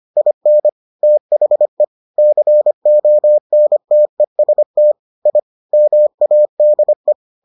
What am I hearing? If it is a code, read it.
Morse code, heard as a keyed tone, about 25 wpm, 600 Hz. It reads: IN THE CONTEST I MADE